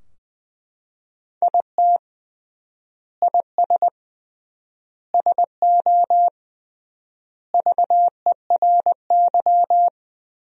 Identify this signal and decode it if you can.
Morse code, heard as a keyed tone, about 21 wpm, 700 Hz. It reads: IT IS SO VERY
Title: 3x Word-Spacing - Sentences from Top 300 Words - 20wpm